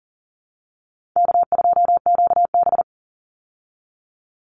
Morse code, heard as a keyed tone, about 40 wpm, 700 Hz. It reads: K2QB
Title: Call Signs Repeated 2x - 40wpm